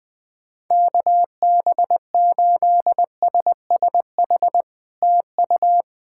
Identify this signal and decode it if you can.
Morse code, heard as a keyed tone, about 20 wpm, 700 Hz. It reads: KB8SSH TU
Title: POTA QSO Practice - 20wpm